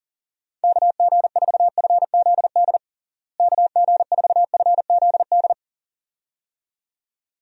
Morse code, heard as a keyed tone, about 37 wpm, 700 Hz. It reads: KG4FZD KG4FZD
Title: USA Call Signs - 6 Characters - Repeated 2x - 40wpm